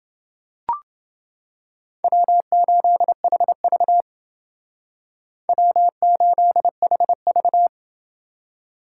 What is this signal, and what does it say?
Morse code, heard as a keyed tone, about 30 wpm, 700 Hz. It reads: W8HV W8HV